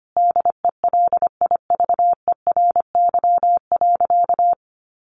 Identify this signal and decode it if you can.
Morse code, heard as a keyed tone, about 25 wpm, 700 Hz. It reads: DELIVERY.
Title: Sentences from Top 1500 Words 25wpm